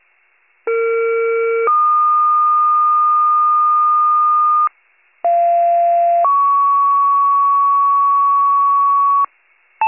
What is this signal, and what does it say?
Two-tone page, heard as a one-second tone, then a three-second tone, about 460 then 1220 Hz, about 690 then 1120 Hz. It is heard from 0.7 to 9.2 s.